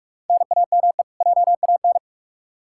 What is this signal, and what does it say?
Morse code, heard as a keyed tone, about 46 wpm, 700 Hz. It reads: NAME JAN
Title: Practice QSO - Exchange 1 - Beginning - 45wpm